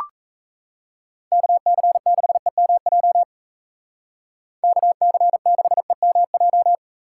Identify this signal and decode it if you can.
Morse code, heard as a keyed tone, about 34 wpm, 700 Hz. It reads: KC6EMJ KC6EMJ
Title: Speed Racing - USA 2x3 Call Signs 28wpm